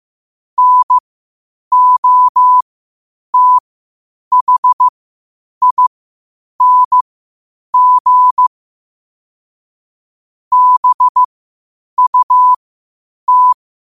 Morse code, heard as a keyed tone, about 15 wpm, spaced as for 8 wpm, 1000 Hz. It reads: NOTHING BUT
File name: qotdfiles / 8wpm